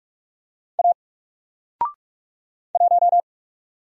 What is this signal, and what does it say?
Morse code, heard as a keyed tone, about 44 wpm, 700 Hz. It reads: A 1
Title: Single Character (Letter:Number:Punctuation:Prosign) 45wpm